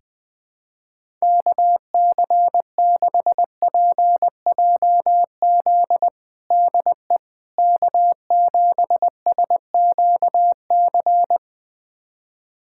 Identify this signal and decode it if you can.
Morse code, heard as a keyed tone, about 20 wpm, 700 Hz. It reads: KC6PJZ DE K7SQC
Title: USA Callsign DE Callsign 20wpm